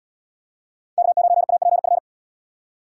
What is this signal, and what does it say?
Morse code, heard as a keyed tone, about 70 wpm, 700 Hz. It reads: K9APF